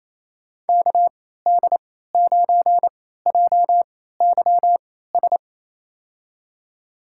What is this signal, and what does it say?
Morse code, heard as a keyed tone, about 28 wpm, spaced as for 16 wpm, 700 Hz. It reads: KD9JYS
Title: USA Call Signs - 6 Characters - 3x Character Spacing - 28wpm